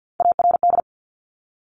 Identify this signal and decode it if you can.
Morse code, heard as a keyed tone, about 50 wpm, 700 Hz. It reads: ARD
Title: Top Three Letter Combinations in Words 50wpm